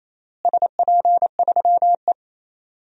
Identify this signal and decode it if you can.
Morse code, heard as a keyed tone, about 28 wpm, 700 Hz. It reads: SP3E
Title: DXCC-100 Call Signs - 4 Characters - Repeated 3x - 28wpm